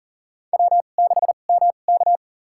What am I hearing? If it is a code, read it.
Morse code, heard as a keyed tone, about 40 wpm, spaced as for 29 wpm, 700 Hz. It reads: W6MK
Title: USA Call Signs - 2x Character Spacing - 40wpm